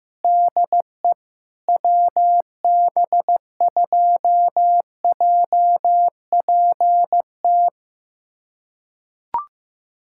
Morse code, heard as a keyed tone, about 15 wpm, 700 Hz. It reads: DE WB2JPT
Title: USA Callsign DE Callsign 15wpm